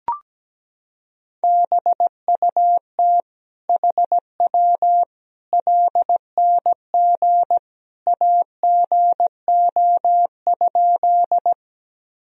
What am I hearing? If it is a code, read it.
Morse code, heard as a keyed tone, about 17 wpm, 700 Hz. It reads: BUT HW LNG AGO?
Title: Sentences from Top 500 Words and Abbreviations 17wpm